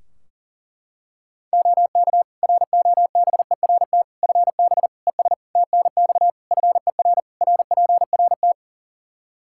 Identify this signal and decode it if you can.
Morse code, heard as a keyed tone, about 40 wpm, 700 Hz. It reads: OK ROBERT FB ES TNX FER RPRT
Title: K1USN SST QSO Practice 40wpm